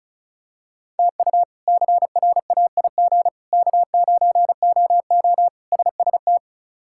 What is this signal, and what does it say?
Morse code, heard as a keyed tone, about 36 wpm, 700 Hz. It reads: TU CRAIG K9OO SST
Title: K1USN SST QSO Practice - 35wpm